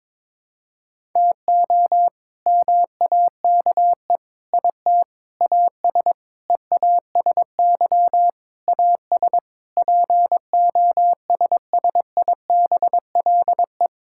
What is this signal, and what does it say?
Morse code, heard as a keyed tone, about 22 wpm, 700 Hz. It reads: TO MAKE IT AS EASY AS POSSIBLE
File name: practice / 22wpm / Sets of 8 Words - 22wpm